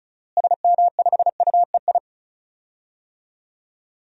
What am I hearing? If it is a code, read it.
Morse code, heard as a keyed tone, about 36 wpm, 700 Hz. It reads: SM5UEI
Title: DXCC-100 Call Signs - 6 Characters - Repeated 2x - 35wpm